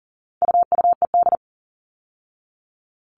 Morse code, heard as a keyed tone, about 39 wpm, 700 Hz. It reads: UUED
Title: All Cut Numbers - 4-Digit 40wpm